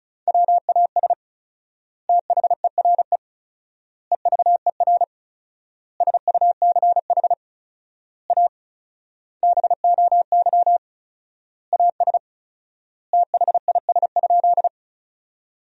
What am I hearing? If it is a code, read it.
Morse code, heard as a keyed tone, about 36 wpm, 700 Hz. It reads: WAS THERE EVER SUCH A BOY AS THIS?
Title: 4x Word-Spacing - Sentences from Top 400 Words - 35wpm